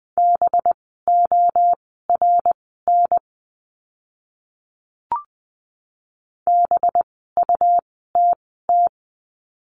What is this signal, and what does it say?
Morse code, heard as a keyed tone, about 20 wpm, spaced as for 14 wpm, 700 Hz. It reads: BORN BUTT
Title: 185 Lesson 30f - Letter b - Words 20wpm